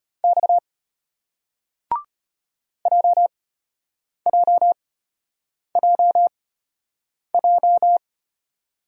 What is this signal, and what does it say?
Morse code, heard as a keyed tone, about 33 wpm, spaced as for 8 wpm, 700 Hz. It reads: X JJJJ